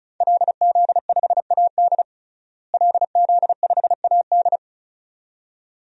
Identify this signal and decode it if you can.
Morse code, heard as a keyed tone, about 36 wpm, 700 Hz. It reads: LZ5AD LZ5AD
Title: DXCC-100 Call Signs - 5 Characters - Repeated 2x - 35wpm